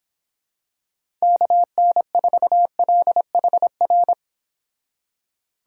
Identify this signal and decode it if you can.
Morse code, heard as a keyed tone, about 26 wpm, 700 Hz. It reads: KN4LHR
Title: Speed Racing - USA Call Signs 22wpm